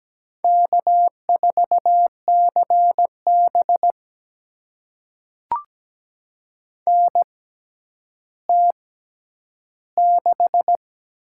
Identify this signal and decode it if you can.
Morse code, heard as a keyed tone, about 17 wpm, spaced as for 10 wpm, 700 Hz. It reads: K4CB NT6
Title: USA Call Signs - 4 Characters - 6x Character Spacing - 17wpm